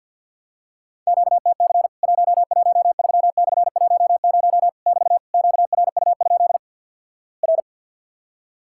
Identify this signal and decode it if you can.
Morse code, heard as a keyed tone, about 50 wpm, 700 Hz. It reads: QTC 113/10 <BT> QRU? R
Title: Worked All Europe DX Contest - QTC Exchange - 50wpm